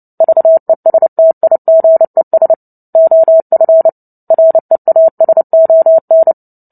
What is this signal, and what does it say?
Morse code, heard as a keyed tone, about 30 wpm, 650 Hz. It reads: VESTIGES OF REASON
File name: Book Two / 03-Chapter 3 - 29 WPM